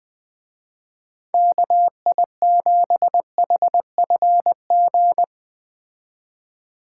Morse code, heard as a keyed tone, about 21 wpm, 700 Hz. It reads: KI7HFG
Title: Speed Racing - USA Call Signs 17wpm